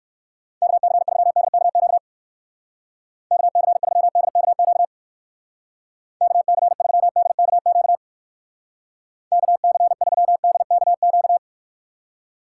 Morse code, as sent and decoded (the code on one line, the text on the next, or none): -.- -.-. ...-- -.. -.- --.- / -.- -.-. ...-- -.. -.- --.- / -.- -.-. ...-- -.. -.- --.- / -.- -.-. ...-- -.. -.- --.-
KC3DKQ KC3DKQ KC3DKQ KC3DKQ